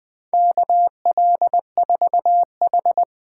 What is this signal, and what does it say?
Morse code, heard as a keyed tone, about 20 wpm, 700 Hz. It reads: KL4H
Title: USA Call Signs - 4 Characters - 2x Character Spacing - 20wpm